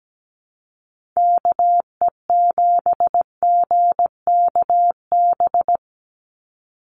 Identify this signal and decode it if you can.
Morse code, heard as a keyed tone, about 17 wpm, 700 Hz. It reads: KE7GKB
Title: USA Call Signs 17wpm